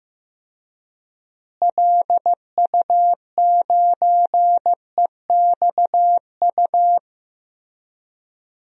Morse code, heard as a keyed tone, about 15 wpm, 700 Hz. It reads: LU9EXU
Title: DXCC-100 Call Signs - 6 Characters - Repeated 3x - 15wpm